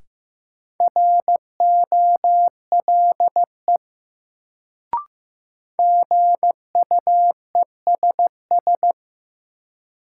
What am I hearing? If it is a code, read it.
Morse code, heard as a keyed tone, about 16 wpm, 700 Hz. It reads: ROLE GUESS
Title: Sentences from Top 1500 Words - New Word Drill Alt - 15wpm